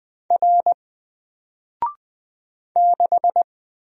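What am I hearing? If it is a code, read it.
Morse code, heard as a keyed tone, about 20 wpm, 700 Hz. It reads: R 6
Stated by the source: Single letter and number Morse drill at 20 wpm, from N4LCD.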